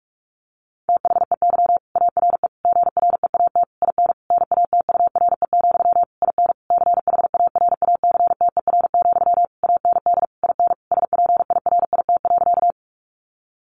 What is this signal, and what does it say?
Morse code, heard as a keyed tone, about 44 wpm, 700 Hz. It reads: THEY ARE GREAT IN NATURE, IN CHARACTER, AND IN SPIRIT.
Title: Sentences from Top 1400 Words 45wpm